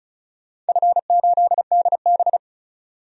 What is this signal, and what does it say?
Morse code, heard as a keyed tone, about 34 wpm, 700 Hz. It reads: F8DB
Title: DXCC-100 Call Signs - 4 Characters - 35wpm